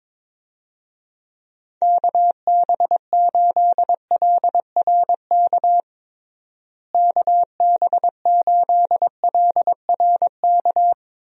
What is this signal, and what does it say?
Morse code, heard as a keyed tone, about 22 wpm, 700 Hz. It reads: KB8LRK KB8LRK
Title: USA Call Signs - 6 Characters - Repeated 3x - 22wpm